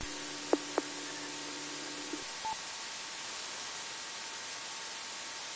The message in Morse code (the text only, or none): none